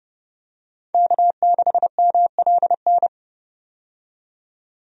Morse code, heard as a keyed tone, about 30 wpm, 700 Hz. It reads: K6MLN
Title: USA Call Signs - 5 Characters - 30wpm